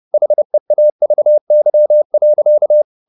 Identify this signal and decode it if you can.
Morse code, heard as a keyed tone, about 30 wpm, 600 Hz. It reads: HEAVY.
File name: the old man and the sea / the old man and the sea 1 30wpm